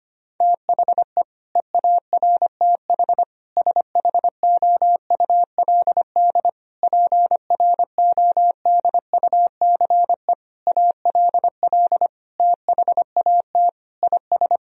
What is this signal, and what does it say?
Morse code, heard as a keyed tone, about 25 wpm, 700 Hz. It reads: THE EARTH SHOULD PRODUCE ALL THAT IS
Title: Sentences from Top 1000 Words - Review Part 3 - 25wpm